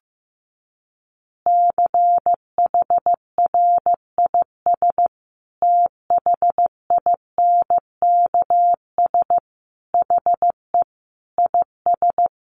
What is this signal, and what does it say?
Morse code, heard as a keyed tone, about 15 wpm, 700 Hz. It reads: CHRIS THINKS HE IS
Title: Easy Phrases 15wpm